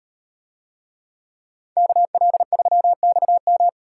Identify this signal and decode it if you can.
Morse code, heard as a keyed tone, about 38 wpm, 700 Hz. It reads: KL3XM